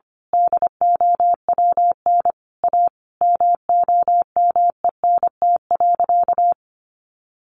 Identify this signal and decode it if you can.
Morse code, heard as a keyed tone, about 25 wpm, 700 Hz. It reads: DOWN A MOMENT.
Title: Sentences from Top 1000 Words 25wpm